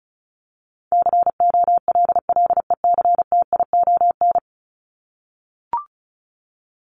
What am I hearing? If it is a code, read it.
Morse code, heard as a keyed tone, about 34 wpm, 700 Hz. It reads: COLLECTION
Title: Top 1001-1250 Words 35wpm